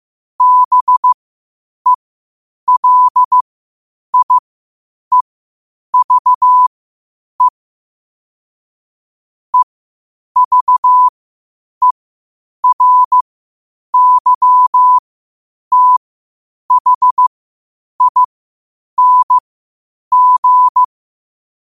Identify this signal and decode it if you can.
Morse code, heard as a keyed tone, about 15 wpm, spaced as for 8 wpm, 1000 Hz. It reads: BELIEVE EVERYTHING
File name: qotdfiles / 8wpm